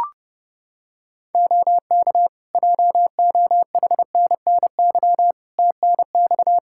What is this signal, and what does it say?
Morse code, heard as a keyed tone, about 30 wpm, 700 Hz. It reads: OK JOHNNY TNX
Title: Practice QSO - Exchange 4 - Ending - 30wpm